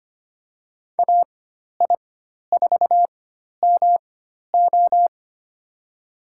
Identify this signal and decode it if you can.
Morse code, heard as a keyed tone, about 25 wpm, spaced as for 12 wpm, 700 Hz. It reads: AI4MO
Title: USA Call Signs - 5 Characters - 4x Character Spacing - 25wpm